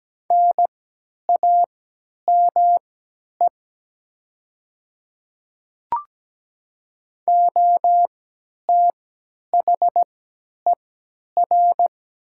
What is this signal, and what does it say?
Morse code, heard as a keyed tone, about 17 wpm, spaced as for 10 wpm, 700 Hz. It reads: NAME OTHER
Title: 3x Character Spacing - Top 200 Words - 17wpm